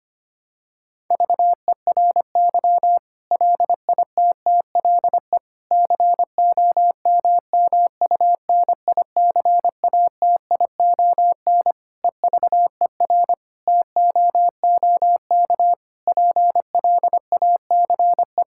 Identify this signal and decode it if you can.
Morse code, heard as a keyed tone, about 25 wpm, 700 Hz. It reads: VERY LITTLE COMMUNICATION EVER TOOK PLACE
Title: Sentences from Top 1000 Words - Review Part 6 - 25wpm